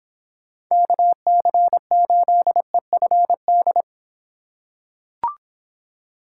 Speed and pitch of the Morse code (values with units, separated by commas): 26 wpm, 700 Hz